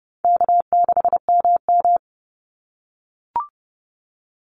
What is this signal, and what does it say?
Morse code, heard as a keyed tone, about 30 wpm, 700 Hz. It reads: K6MM
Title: Call Signs - 30wpm